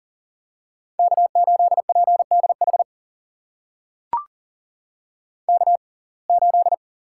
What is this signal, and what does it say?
Morse code, heard as a keyed tone, about 40 wpm, spaced as for 29 wpm, 700 Hz. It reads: K8PDH K8
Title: USA Call Signs - 5 Characters - 6x Character Spacing - 40wpm